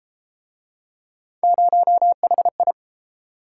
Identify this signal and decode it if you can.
Morse code, heard as a keyed tone, about 29 wpm, 700 Hz. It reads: 0HI